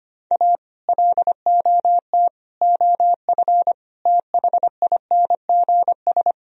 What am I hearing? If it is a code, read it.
Morse code, heard as a keyed tone, about 31 wpm, 700 Hz. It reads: A LOT OF THINGS